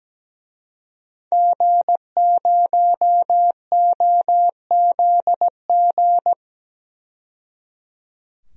Morse code, heard as a keyed tone, about 17 wpm, 700 Hz. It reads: G0OZG
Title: DXCC-100 Call Signs - 5 Characters - 17wpm